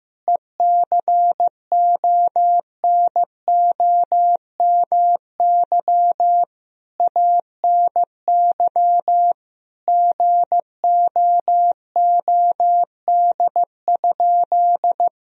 Morse code, heard as a keyed tone, about 15 wpm, 700 Hz. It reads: ECONOMY ANY GOOD?
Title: Sentences from Top 1000 Words - Review Part 4 - 15wpm